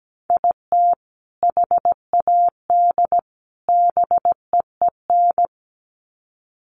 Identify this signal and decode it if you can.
Morse code, heard as a keyed tone, about 17 wpm, 700 Hz. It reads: IT HAD BEEN
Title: Sets of 3 Words 17wpm